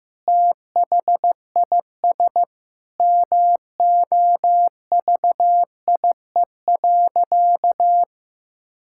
Morse code, heard as a keyed tone, about 15 wpm, 700 Hz. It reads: THIS MOVIE.
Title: Sentences from Top 1000 Words - Review Part 5 - 15wpm